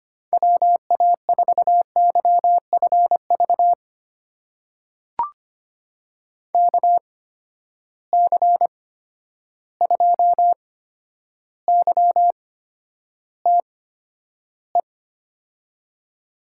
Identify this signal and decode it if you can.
Morse code, heard as a keyed tone, about 25 wpm, spaced as for 11 wpm, 700 Hz. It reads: WA4YFV KC2YTE
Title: USA Call Signs - 8x Character Spacing - 25wpm